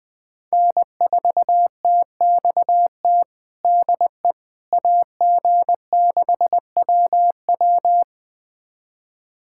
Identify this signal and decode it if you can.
Morse code, heard as a keyed tone, about 20 wpm, 700 Hz. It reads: N4TXT DE AG6WW
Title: USA Callsign DE Callsign - 5 Characters - 20wpm